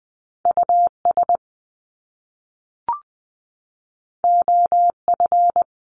Morse code, heard as a keyed tone, about 20 wpm, 700 Hz. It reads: US OF
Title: Top Two Letter Combinations in Words 20wpm